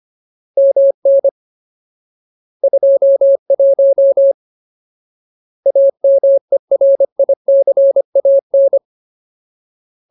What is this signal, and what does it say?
Morse code, heard as a keyed tone, about 25 wpm, 550 Hz. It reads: MN 21 AMERICAN
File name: Ultimate Word List / 25WPM Ultimate